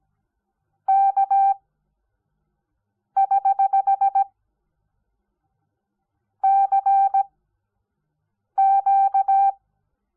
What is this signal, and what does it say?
Morse code, heard as a keyed tone, about 17 wpm, spaced as for 5 wpm, 780 Hz. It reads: K<HH> CQ
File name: Onlinelehrgang / morsen / DJ4UF Morsen05